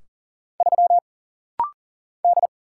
Morse code, heard as a keyed tone, about 40 wpm, 700 Hz. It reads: 3 D
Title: Single Letter-Number - Rapid-Fire 40wpm